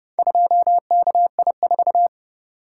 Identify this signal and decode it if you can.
Morse code, heard as a keyed tone, about 30 wpm, 700 Hz. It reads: 2KI4